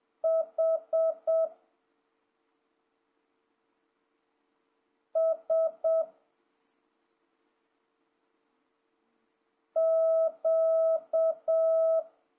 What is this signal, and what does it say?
Morse code, heard as a keyed tone, about 7 wpm, 650 Hz. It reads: H S Q
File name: jh8cbh / 2026cw